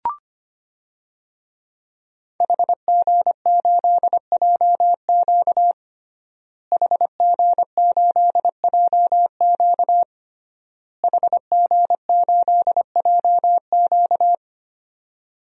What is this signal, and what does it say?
Morse code, heard as a keyed tone, about 25 wpm, 700 Hz. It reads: HG8JQ HG8JQ HG8JQ
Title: DXCC-100 Call Signs - 5 Characters - Repeated 3x - 25wpm